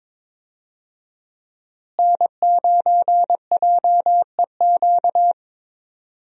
Morse code, heard as a keyed tone, about 22 wpm, 700 Hz. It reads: N9JEQ
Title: USA Call Signs - 5 Characters - Repeated 3x - 22wpm